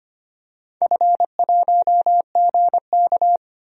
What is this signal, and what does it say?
Morse code, heard as a keyed tone, about 25 wpm, 700 Hz. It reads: F1GK
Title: DXCC-100 Call Signs - 4 Characters - 25wpm